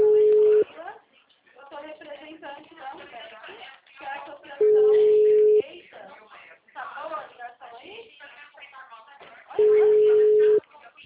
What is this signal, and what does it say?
Ringback tone (Europe, 425 Hz), with speech.
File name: out- -2027